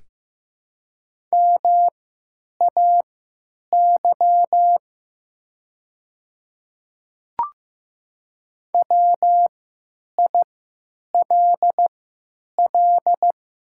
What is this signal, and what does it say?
Morse code, heard as a keyed tone, about 15 wpm, spaced as for 9 wpm, 700 Hz. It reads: MAY WILL
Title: 3x Character Spacing - Top 200 Words - 15wpm